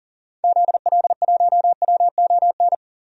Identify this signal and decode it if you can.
Morse code, heard as a keyed tone, about 40 wpm, 700 Hz. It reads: ZL1WON
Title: DXCC-100 Call Signs - 6 Characters - 40wpm